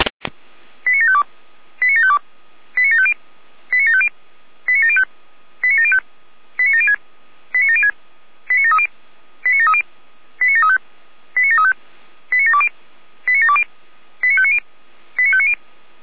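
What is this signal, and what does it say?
Five-tone selective call (ZVEI), heard as quick runs of five tones, tones of about 70 ms. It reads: 89752 89752 89750 89750 89705 89705 89706 89706 89730 89730 89735 89735 89720 89720 89590 89590
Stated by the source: Received on SDR (ZVEI).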